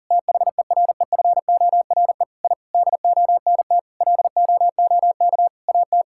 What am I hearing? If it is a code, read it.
Morse code, heard as a keyed tone, about 40 wpm, 700 Hz. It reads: THEREFORE I DONT LOOK AT
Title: Sentences from Top 1000 Words - Review Part 5 - 40wpm